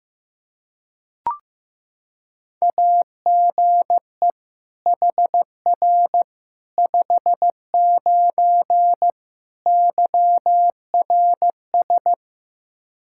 Morse code, heard as a keyed tone, about 15 wpm, 700 Hz. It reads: AGE HR 59 YRS
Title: K1USN SST QSO Practice 15wpm